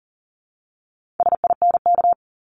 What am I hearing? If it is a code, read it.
Morse code, heard as a keyed tone, about 41 wpm, 700 Hz. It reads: SINK
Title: Multi-letter Phonemes Repeated 3x 40wpm